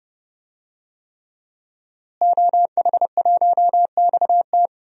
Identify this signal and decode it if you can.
Morse code, heard as a keyed tone, about 30 wpm, 700 Hz. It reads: OH1XT
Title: DXCC-100 Call Signs - 5 Characters - Repeated 3x - 30wpm